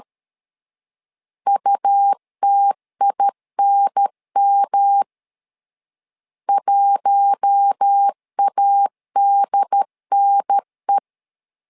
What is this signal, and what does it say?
Morse code, heard as a keyed tone, about 13 wpm, 800 Hz. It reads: UTINM 1ADNE
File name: LEZIONI / 4 Q1 12wpm 2028